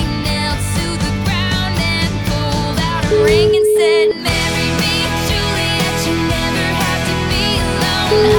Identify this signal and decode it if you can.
Ringback tone (Europe, 425 Hz), with noise.